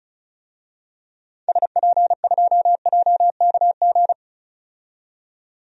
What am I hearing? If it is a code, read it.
Morse code, heard as a keyed tone, about 34 wpm, 700 Hz. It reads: SP2JKG
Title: DXCC-100 Call Signs - 6 Characters - 35wpm